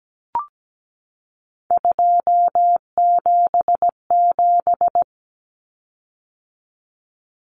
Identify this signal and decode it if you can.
Morse code, heard as a keyed tone, about 17 wpm, 700 Hz. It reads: 277
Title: Common Cut Numbers - 3-Digit 17wpm